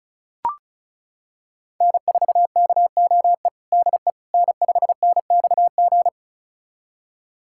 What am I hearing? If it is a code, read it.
Morse code, heard as a keyed tone, about 36 wpm, 700 Hz. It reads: N4KOE DE N5NXG
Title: USA Callsign DE Callsign - 5 Characters - 35wpm